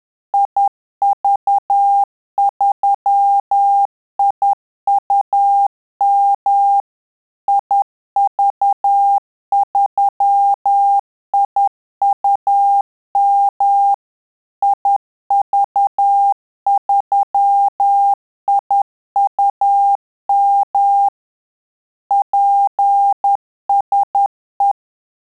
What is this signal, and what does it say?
Morse code, heard as a keyed tone, about 10 wpm, 800 Hz. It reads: IV3IUM IV3IUM IV3IUM PSE
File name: Telegraphy-group / QSO CW 10 PPM